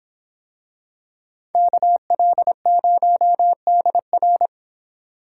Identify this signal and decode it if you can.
Morse code, heard as a keyed tone, about 23 wpm, 700 Hz. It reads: KL0DR